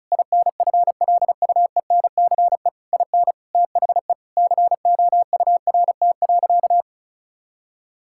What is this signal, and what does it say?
Morse code, heard as a keyed tone, about 34 wpm, 700 Hz. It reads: INFLUENCE IN THE COURT.
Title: Sentences from Top 1600 Words 35wpm